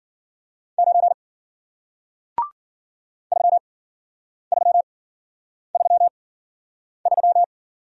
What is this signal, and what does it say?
Morse code, heard as a keyed tone, about 52 wpm, spaced as for 9 wpm, 700 Hz. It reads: 9 3333